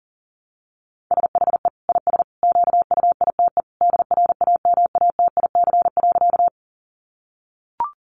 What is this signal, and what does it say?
Morse code, heard as a keyed tone, about 40 wpm, 700 Hz. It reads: SHE IS QUITE DRAMATIC.
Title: Sentences from Top 1300 Words 40wpm